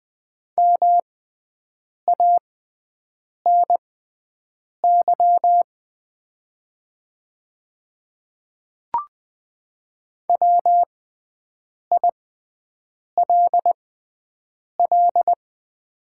Morse code, heard as a keyed tone, about 20 wpm, spaced as for 7 wpm, 700 Hz. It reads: MANY WILL
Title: 6x Character Spacing - Top 100 Words - 20wpm